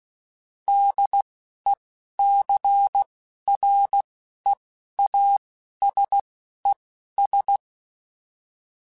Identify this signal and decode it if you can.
Morse code, heard as a keyed tone, about 16 wpm, spaced as for 11 wpm, 800 Hz. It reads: DECREASES
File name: Technical Subjects / Technical Subjects-mp3s / XL @10